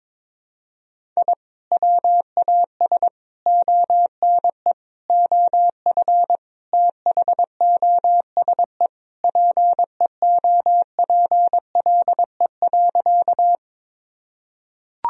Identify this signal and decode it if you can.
Morse code, heard as a keyed tone, about 22 wpm, 700 Hz. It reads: I WAS ONE OF THOSE PEOPLE.
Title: Sentences from Top 1000 Words - Review Part 1 - 22wpm